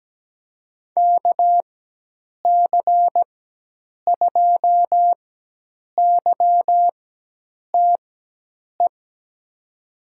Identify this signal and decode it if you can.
Morse code, heard as a keyed tone, about 17 wpm, spaced as for 8 wpm, 700 Hz. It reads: KC2YTE
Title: USA Call Signs - 6 Characters - 4x Character Spacing - 17wpm